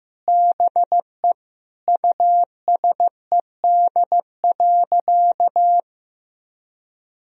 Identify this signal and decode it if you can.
Morse code, heard as a keyed tone, about 15 wpm, 700 Hz. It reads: BE USED.
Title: Sentences from Top 1000 Words - Review Part 4 - 15wpm